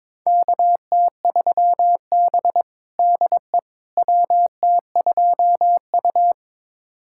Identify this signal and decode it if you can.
Morse code, heard as a keyed tone, about 21 wpm, 700 Hz. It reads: KT3B DE WT2U
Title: USA Callsign DE Callsign - 4 Characters - 22wpm